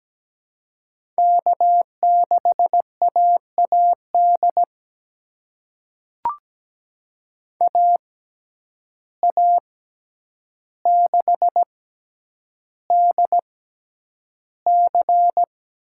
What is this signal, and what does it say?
Morse code, heard as a keyed tone, about 17 wpm, spaced as for 9 wpm, 700 Hz. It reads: K6AAD AA6DC
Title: USA Call Signs - 6x Character Spacing - 17wpm